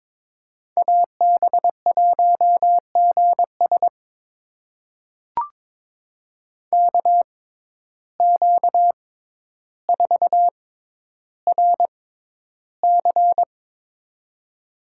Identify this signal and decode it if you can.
Morse code, heard as a keyed tone, about 22 wpm, spaced as for 11 wpm, 700 Hz. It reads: AB1GS KQ4RC
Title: USA Call Signs - 5 Characters - 6x Character Spacing - 22wpm